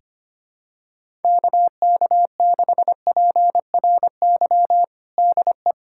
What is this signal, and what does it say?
Morse code, heard as a keyed tone, about 25 wpm, 700 Hz. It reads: KK6PRY DE
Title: USA Callsign DE Callsign - 6 Characters - 25wpm